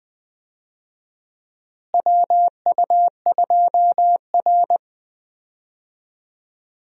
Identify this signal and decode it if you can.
Morse code, heard as a keyed tone, about 20 wpm, 700 Hz. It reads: WU2R
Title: USA Call Signs - 4 Characters - Repeated 2x - 20wpm